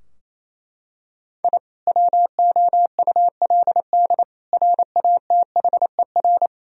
Morse code, heard as a keyed tone, about 28 wpm, 700 Hz. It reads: I WOULD RATHER
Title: Sentences from Top 1000 Words - Review Part 4 - 28wpm